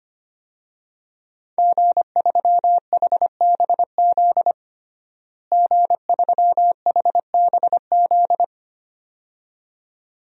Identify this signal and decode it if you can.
Morse code, heard as a keyed tone, about 25 wpm, 700 Hz. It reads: G3HBZ G3HBZ
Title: DXCC-100 Call Signs - 5 Characters - Repeated 2x - 25wpm